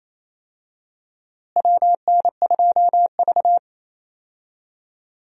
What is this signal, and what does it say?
Morse code, heard as a keyed tone, about 28 wpm, 700 Hz. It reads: WN2V
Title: USA Call Signs - 4 Characters - Repeated 3x - 28wpm